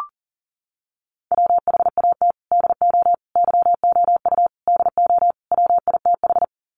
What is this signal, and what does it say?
Morse code, heard as a keyed tone, about 40 wpm, 700 Hz. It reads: WHAT DO YOU DO WITH
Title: Sets of 5 Words Encore2 40wpm